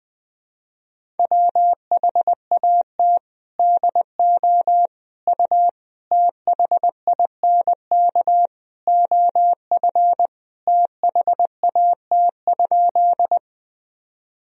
Morse code, heard as a keyed tone, about 20 wpm, 700 Hz. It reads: WHAT DO U THINK OF THAT?
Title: Sentences from Top 200 Words and Abbreviations 20wpm